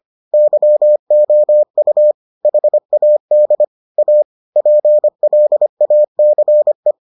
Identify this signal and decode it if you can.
Morse code, heard as a keyed tone, about 25 wpm, 600 Hz. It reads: YOU HAD A PLACE